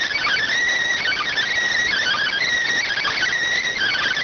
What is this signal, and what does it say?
SSTV (slow-scan television), PD 290.